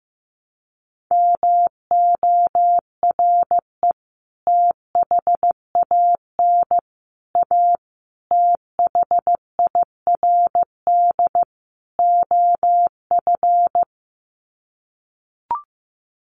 Morse code, heard as a keyed tone, about 15 wpm, 700 Hz. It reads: MORE THAN A THIRD OF
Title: Sets of 5 Words Encore2 15wpm